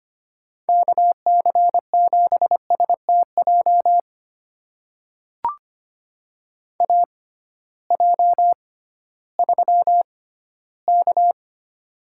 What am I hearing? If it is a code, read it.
Morse code, heard as a keyed tone, about 25 wpm, spaced as for 15 wpm, 700 Hz. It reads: KC7STJ AJ3K
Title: USA Call Signs - 6x Character Spacing - 25wpm